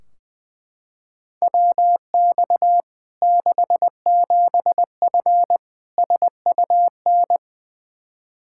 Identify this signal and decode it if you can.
Morse code, heard as a keyed tone, about 20 wpm, 700 Hz. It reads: WX 67F SUN
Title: K1USN SST QSO Practice 20wpm